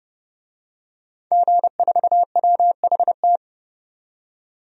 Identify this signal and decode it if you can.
Morse code, heard as a keyed tone, about 30 wpm, 700 Hz. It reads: G4WHT
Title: DXCC-100 Call Signs - 5 Characters - 30wpm